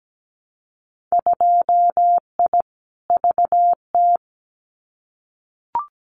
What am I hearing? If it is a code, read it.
Morse code, heard as a keyed tone, about 17 wpm, 700 Hz. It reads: 2I VT
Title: Winter Field Day Exchanges 17wpm